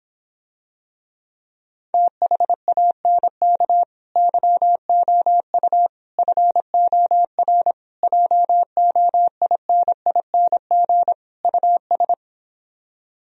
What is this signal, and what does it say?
Morse code, heard as a keyed tone, about 26 wpm, 700 Hz. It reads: THANK YOU FOR JOINING US